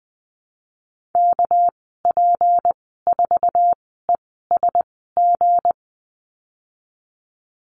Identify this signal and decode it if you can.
Morse code, heard as a keyed tone, about 20 wpm, spaced as for 14 wpm, 700 Hz. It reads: KP4ESG
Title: 200 Lesson 32g - Letter k - Call Signs 20wpm